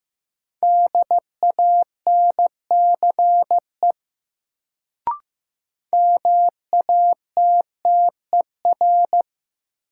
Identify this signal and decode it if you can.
Morse code, heard as a keyed tone, about 16 wpm, 700 Hz. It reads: DANCE MATTER
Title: Sentences from Top 700 Words - New Word Drill - 15wpm